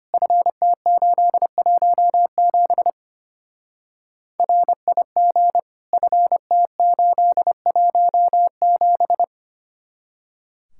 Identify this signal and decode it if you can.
Morse code, heard as a keyed tone, about 30 wpm, 700 Hz. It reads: FT817 RIG FT817
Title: Speed-Racing Common QSO Phrases - 25wpm